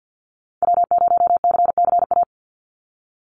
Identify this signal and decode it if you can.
Morse code, heard as a keyed tone, about 50 wpm, 700 Hz. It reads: W0CCA
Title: NA SOTA Activator Callsigns 50wpm